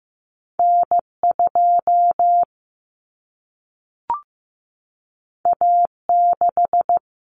Morse code, heard as a keyed tone, about 15 wpm, 700 Hz. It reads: N2 A6
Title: Common Cut Numbers - 2-Digit 15wpm